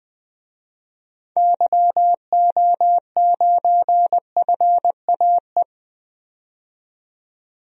Morse code, heard as a keyed tone, about 20 wpm, 700 Hz. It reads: YO9FAE
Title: DXCC-100 Call Signs - 6 Characters - 20wpm